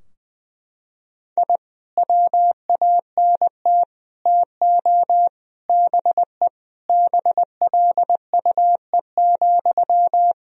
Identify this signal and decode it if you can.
Morse code, heard as a keyed tone, about 20 wpm, 700 Hz. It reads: I WANT TO BE BLUE,
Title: Sentences from Top 1000 Words - Review Part 3 - 20wpm